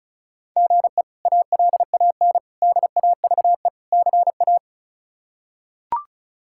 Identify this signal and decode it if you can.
Morse code, heard as a keyed tone, about 34 wpm, 700 Hz. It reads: GE ALAN DAVE CA
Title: K1USN SST QSO Practice - 35wpm